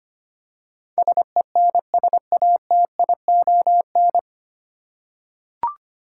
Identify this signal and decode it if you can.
Morse code, heard as a keyed tone, about 25 wpm, 700 Hz. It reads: SENSATION
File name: practice / 25wpm / 9 Letter Words - First 3 Letters Given - 25wpm